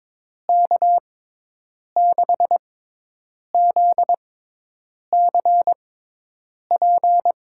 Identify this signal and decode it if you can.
Morse code, heard as a keyed tone, about 22 wpm, spaced as for 8 wpm, 700 Hz. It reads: K6ZCP
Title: USA Call Signs - 6 Characters - 6x Character Spacing - 22wpm